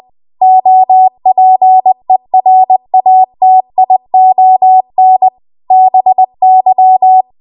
Morse code, heard as a keyed tone, about 20 wpm, 750 Hz. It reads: OPERATION BY
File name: Archive / 20 WPM / 20WPM